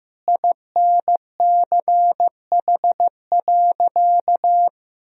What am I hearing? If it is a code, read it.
Morse code, heard as a keyed tone, about 15 wpm, 700 Hz. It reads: INCH.
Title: Sentences from Top 1000 Words - Review Part 3 - 15wpm